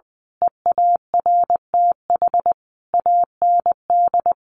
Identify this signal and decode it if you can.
Morse code, heard as a keyed tone, about 20 wpm, 700 Hz. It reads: EARTH AND
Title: A Sound Of Thunder 20wpm